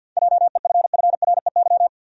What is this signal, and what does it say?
Morse code, heard as a keyed tone, about 52 wpm, 700 Hz. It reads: JEFFREY
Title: USA Common Names - 50wpm